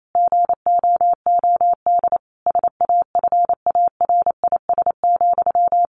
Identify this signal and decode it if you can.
Morse code, heard as a keyed tone, about 28 wpm, 700 Hz. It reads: GOOD SAFARIS,
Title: A Sound Of Thunder 28wpm